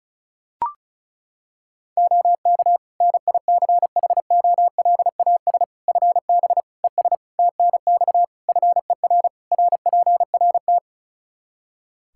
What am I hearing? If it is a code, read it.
Morse code, heard as a keyed tone, about 34 wpm, 700 Hz. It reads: OK NICHOLAS FB ES TNX FER RPRT
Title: Practice QSO - Exchange 2 - Info - 35wpm